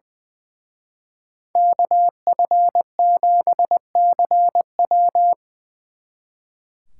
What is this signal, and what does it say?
Morse code, heard as a keyed tone, about 23 wpm, 700 Hz. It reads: KF7CW
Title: Speed Racing - USA 2x2 Call Signs 20wpm